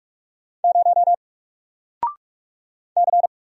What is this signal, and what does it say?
Morse code, heard as a keyed tone, about 44 wpm, 700 Hz. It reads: 0 C
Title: Single Character (Letter:Number:Punctuation:Prosign) Alt 45wpm